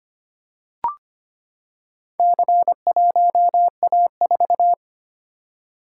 Morse code, heard as a keyed tone, about 25 wpm, 700 Hz. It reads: C1A4
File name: practice / 25wpm / Four Letter-Number Combinations - 25wpm